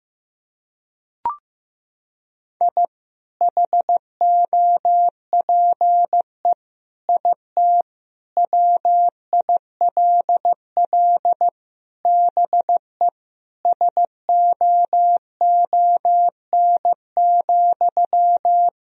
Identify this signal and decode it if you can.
Morse code, heard as a keyed tone, about 15 wpm, 700 Hz. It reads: I HOPE IT WILL BE SOON,
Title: Sentences from Top 1000 Words - Review Part 2 - 15wpm